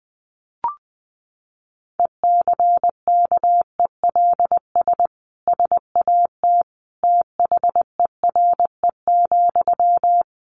Morse code, heard as a keyed tone, about 19 wpm, 700 Hz. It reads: ECKELS SAT THERE,
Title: A Sound Of Thunder 20wpm